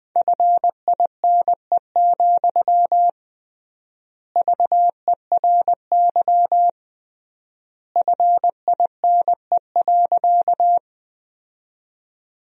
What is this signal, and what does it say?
Morse code, heard as a keyed tone, about 20 wpm, 700 Hz. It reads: FINE, VERY FINE.
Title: 3x Word-Spacing - Sentences from Top 500 Words - 20wpm